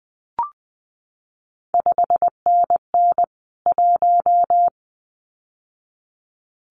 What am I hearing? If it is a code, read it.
Morse code, heard as a keyed tone, about 20 wpm, 700 Hz. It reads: 5NN 1
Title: CQ WW DX Contest Exchange Encore - 20wpm